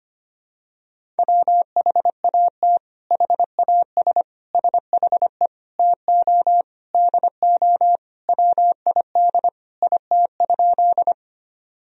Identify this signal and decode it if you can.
Morse code, heard as a keyed tone, about 25 wpm, 700 Hz. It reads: WHAT HAS SHE TO DO WID IT?
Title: Sentences from Top 100 Words and Abbreviations 25wpm